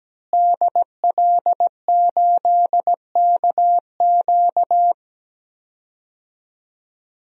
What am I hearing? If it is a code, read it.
Morse code, heard as a keyed tone, about 17 wpm, 700 Hz. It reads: DL8KQ
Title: DXCC-100 Call Signs - 5 Characters - Repeated 3x - 17wpm